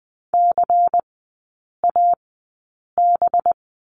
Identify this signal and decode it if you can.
Morse code, heard as a keyed tone, about 20 wpm, spaced as for 8 wpm, 700 Hz. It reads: CAB
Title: 191 Lesson 31e - Letter v - Three Characters 20wpm